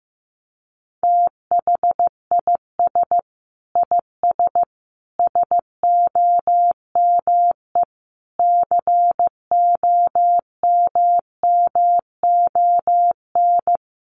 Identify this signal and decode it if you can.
Morse code, heard as a keyed tone, about 15 wpm, 700 Hz. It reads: THIS IS SOME COMMON
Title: Sentences from Top 1400 Words 15wpm